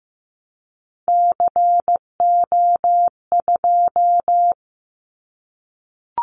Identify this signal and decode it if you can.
Morse code, heard as a keyed tone, about 15 wpm, 700 Hz. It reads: CO2
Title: Callsign Prefixes for Top 100 DXCC - 15wpm